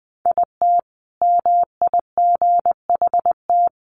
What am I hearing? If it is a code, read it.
Morse code, heard as a keyed tone, about 20 wpm, 700 Hz. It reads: IT MIGHT
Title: Sentences from Top 300 Words 20wpm